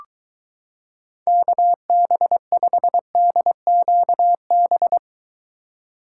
Morse code, heard as a keyed tone, about 23 wpm, 700 Hz. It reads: KB5DQB